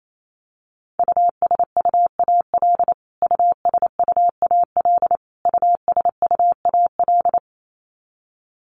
Morse code, heard as a keyed tone, about 28 wpm, 700 Hz. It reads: USUAL USUAL USUAL
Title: Top 500 Words Repeated 28wpm